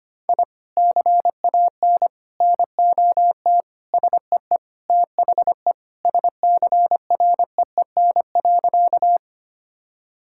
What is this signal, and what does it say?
Morse code, heard as a keyed tone, about 25 wpm, 700 Hz. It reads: I CAN NOT SEE THE SCREEN.
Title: Sentences from Top 1000 Words - Review Part 6 - 25wpm